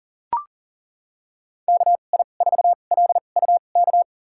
Morse code, heard as a keyed tone, about 37 wpm, spaced as for 28 wpm, 700 Hz. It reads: KI4LUK
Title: USA Call Signs - 2x Character Spacing - 40wpm